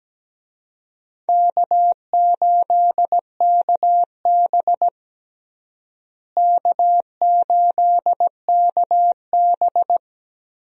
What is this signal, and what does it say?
Morse code, heard as a keyed tone, about 17 wpm, 700 Hz. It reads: K8KB K8KB
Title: USA Call Signs - 4 Characters - Repeated 3x - 17wpm